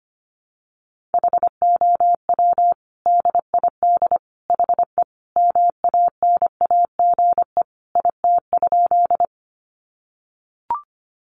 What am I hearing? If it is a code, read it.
Morse code, heard as a keyed tone, about 25 wpm, 700 Hz. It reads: HOW DID HE MANAGE IT?
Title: Sentences from Top 1400 Words 25wpm